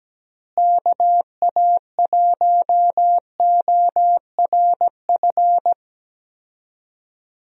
Morse code, heard as a keyed tone, about 17 wpm, 700 Hz. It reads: KA1ORF
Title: USA Call Signs - 6 Characters - Repeated 3x - 17wpm